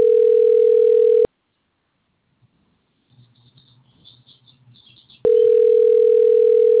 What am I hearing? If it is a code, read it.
Ringback tone (North America), 440 + 480 Hz.